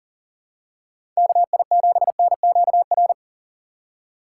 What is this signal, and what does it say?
Morse code, heard as a keyed tone, about 38 wpm, 700 Hz. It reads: KI7NQR
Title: USA Call Signs - 6 Characters - 3x Character Spacing - 40wpm